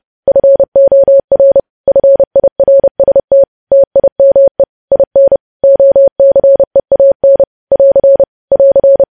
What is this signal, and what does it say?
Morse code, heard as a keyed tone, about 30 wpm, 550 Hz. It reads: FOR FIRST TIME IN OCEAN <AR> <AR>
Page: news headlines in morse code at 30 wpm